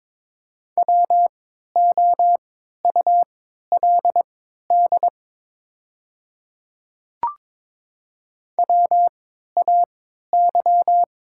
Morse code, heard as a keyed tone, about 23 wpm, spaced as for 13 wpm, 700 Hz. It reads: WOULD WAY
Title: 3x Character Spacing - Top 100 Words - 22wpm